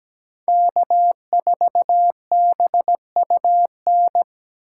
Morse code, heard as a keyed tone, about 17 wpm, 700 Hz. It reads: K4BUN
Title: USA Call Signs - 5 Characters - 2x Character Spacing - 17wpm